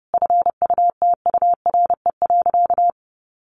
Morse code, heard as a keyed tone, about 30 wpm, 700 Hz. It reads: FUTURE.
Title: Sentences from Top 1500 Words 30wpm